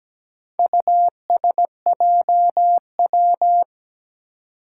Morse code, heard as a keyed tone, about 17 wpm, 700 Hz. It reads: USJW
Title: Four Challenging Letter-Number Combinations 17wpm